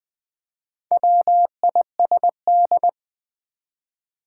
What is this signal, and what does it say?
Morse code, heard as a keyed tone, about 20 wpm, 690 Hz. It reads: WISD
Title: Build A Word - 2 Letters At A Time - 6-Letter Words - 20wpm